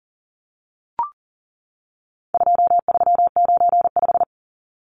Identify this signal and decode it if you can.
Morse code, heard as a keyed tone, about 40 wpm, 700 Hz. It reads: 2395
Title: Numbers - 4-Digit 40wpm